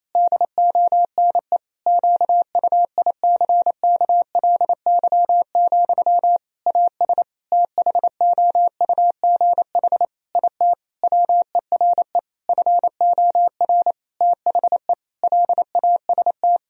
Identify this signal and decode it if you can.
Morse code, heard as a keyed tone, about 27 wpm, 700 Hz. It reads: DONE QUICKLY, AS THOUGH IT WERE FOR THE LAST
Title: Sentences from Top 1000 Words - Review Part 2 - 28wpm